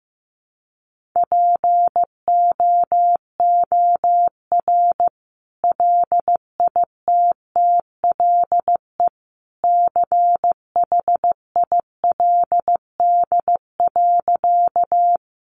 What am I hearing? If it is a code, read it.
Morse code, heard as a keyed tone, about 15 wpm, 700 Hz. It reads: POOR LITTLE CHILD.
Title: Sentences from Top 1000 Words 15wpm